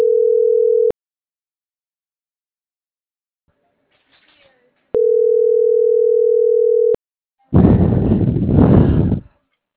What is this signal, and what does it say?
Ringback tone (North America), 440 + 480 Hz.